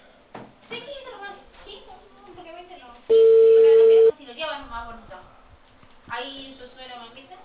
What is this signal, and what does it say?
Ringback tone (China, 450 Hz), with speech.